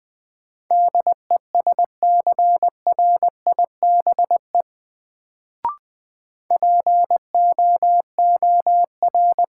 Morse code, heard as a keyed tone, about 20 wpm, 700 Hz. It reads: DESCRIBE POOR
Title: Sentences from Top 1000 Words - New Word Drill - 20wpm